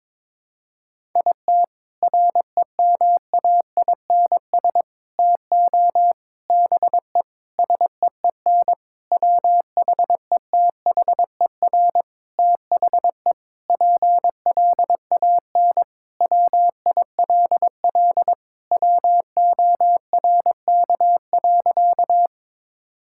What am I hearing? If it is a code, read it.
Morse code, heard as a keyed tone, about 22 wpm, 700 Hz. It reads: IT REMAINS TO BE SEEN WHETHER THE PLAN WILL WORK.
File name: Sentences with 7-Word Phrases - 22wpm